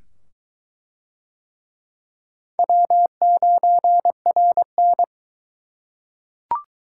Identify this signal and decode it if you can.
Morse code, heard as a keyed tone, about 20 wpm, 700 Hz. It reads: W9RN